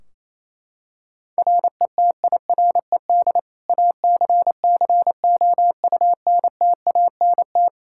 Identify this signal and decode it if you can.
Morse code, heard as a keyed tone, about 28 wpm, 700 Hz. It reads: RETIRED ACCOUNTANT
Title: Practice QSO - Exchange 3 - More Info - 28wpm